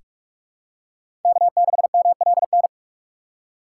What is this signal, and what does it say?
Morse code, heard as a keyed tone, about 46 wpm, 700 Hz. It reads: K6MLN
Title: USA Call Signs - 4x Character Spacing - 45wpm